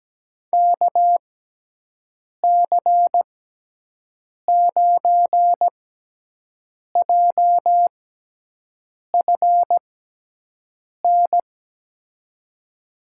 Morse code, heard as a keyed tone, about 17 wpm, spaced as for 6 wpm, 700 Hz. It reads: KC9JFN
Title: USA Call Signs - 6 Characters - 6x Character Spacing - 17wpm